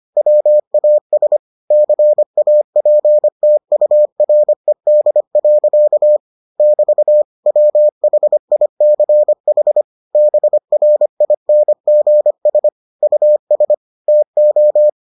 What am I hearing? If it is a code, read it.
Morse code, heard as a keyed tone, about 25 wpm, 600 Hz. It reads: WAS CAPTURED. <BT> WHICH BRINGS US TO